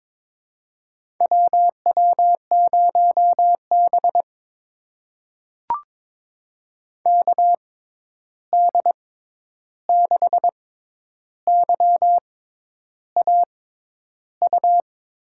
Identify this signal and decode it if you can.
Morse code, heard as a keyed tone, about 22 wpm, spaced as for 10 wpm, 700 Hz. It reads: WW0B KD6YAU
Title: USA Call Signs - 6x Character Spacing - 22wpm